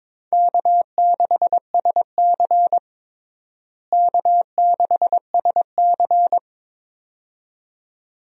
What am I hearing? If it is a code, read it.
Morse code, heard as a keyed tone, about 22 wpm, 700 Hz. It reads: K6SC K6SC
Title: USA Call Signs - 4 Characters - Repeated 2x - 22wpm